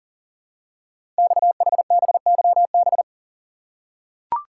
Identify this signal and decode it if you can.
Morse code, heard as a keyed tone, about 42 wpm, 690 Hz. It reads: XHBYB
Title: Code Groups - 5 Letters - 40wpm